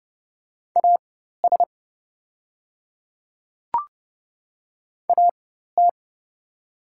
Morse code, heard as a keyed tone, about 30 wpm, spaced as for 14 wpm, 700 Hz. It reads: AS AT
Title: 4x Character Spacing - Top 100 Words - 30wpm